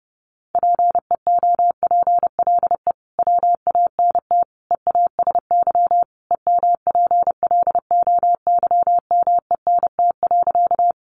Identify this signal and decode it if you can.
Morse code, heard as a keyed tone, about 30 wpm, 700 Hz. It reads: PEOPLE WANT EASY EMPLOYMENT.
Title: Sentences from Top 1500 Words 30wpm